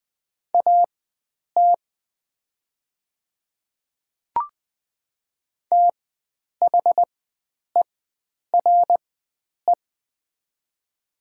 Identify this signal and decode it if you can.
Morse code, heard as a keyed tone, about 20 wpm, spaced as for 9 wpm, 690 Hz. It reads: AT THERE
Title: 4x Character Spacing - Top 100 Words - 20wpm